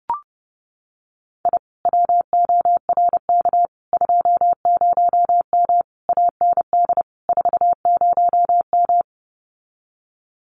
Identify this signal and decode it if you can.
Morse code, heard as a keyed tone, about 30 wpm, 700 Hz. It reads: I WORK 20M AND 40M
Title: Common QSO Phrases Encore 30wpm